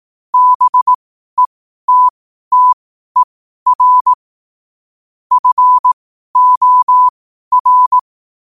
Morse code, heard as a keyed tone, about 18 wpm, spaced as for 12 wpm, 1000 Hz. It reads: BETTER FOR
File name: qotdfiles / 12wpm